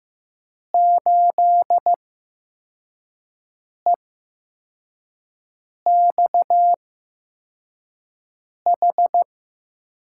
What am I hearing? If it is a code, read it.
Morse code, heard as a keyed tone, about 16 wpm, spaced as for 4 wpm, 700 Hz. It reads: 8EXH
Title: USA Call Signs - 5 Characters - 8x Character Spacing - 15wpm